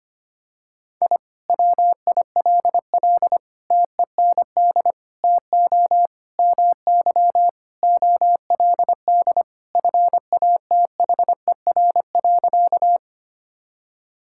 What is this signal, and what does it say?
Morse code, heard as a keyed tone, about 25 wpm, 700 Hz. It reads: I WILL TEND TO MY OLD FATHER.
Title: Sentences from Top 1000 Words - Review Part 5 - 25wpm